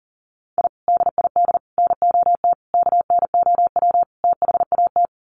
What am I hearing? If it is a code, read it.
Morse code, heard as a keyed tone, about 40 wpm, 700 Hz. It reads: I DID NOT KNOW THAT
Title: Sets of 5 Words 40wpm